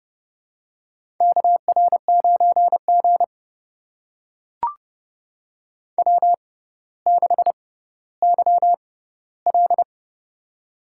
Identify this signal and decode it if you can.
Morse code, heard as a keyed tone, about 30 wpm, spaced as for 15 wpm, 700 Hz. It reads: KR9G W6YL
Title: USA Call Signs - 4 Characters - 6x Character Spacing - 30wpm